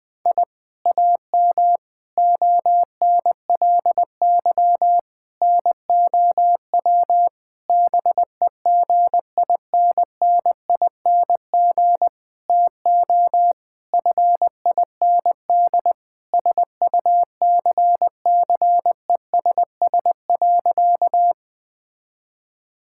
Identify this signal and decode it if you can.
Morse code, heard as a keyed tone, about 20 wpm, 700 Hz. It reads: I AM ONLY NOW BEGINNING TO FIND SUCCESS.
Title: Sentences from Top 1000 Words - Review Part 5 - 20wpm